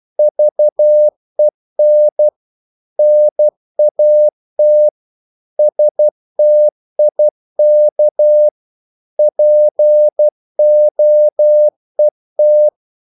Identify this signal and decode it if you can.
Morse code, heard as a keyed tone, about 12 wpm, 600 Hz. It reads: VEN NAT STIK POET
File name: Korte ord DK 12wpm